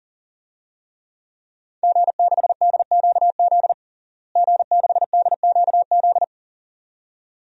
Morse code, heard as a keyed tone, about 40 wpm, 700 Hz. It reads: G6DQZ G6DQZ
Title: DXCC-100 Call Signs - 5 Characters - Repeated 2x - 40wpm